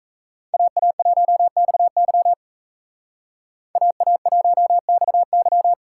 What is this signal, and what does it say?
Morse code, heard as a keyed tone, about 34 wpm, 700 Hz. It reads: AA1XY AA1XY
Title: Speed Racing - USA 2x2 Call Signs 28wpm